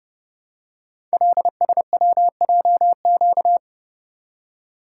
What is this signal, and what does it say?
Morse code, heard as a keyed tone, about 29 wpm, 700 Hz. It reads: LSWJQ
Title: Code Groups - 5 Letter:Numbers - 30wpm